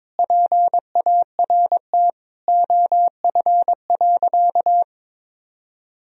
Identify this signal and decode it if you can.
Morse code, heard as a keyed tone, about 22 wpm, 700 Hz. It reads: PART OF.
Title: Sentences from Top 1000 Words - Review Part 1 - 22wpm